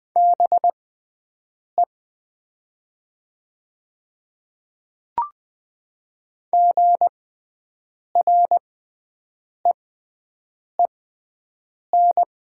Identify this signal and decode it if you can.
Morse code, heard as a keyed tone, about 20 wpm, spaced as for 7 wpm, 700 Hz. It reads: BE GREEN
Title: 6x Character Spacing - Top 500 Words - 20wpm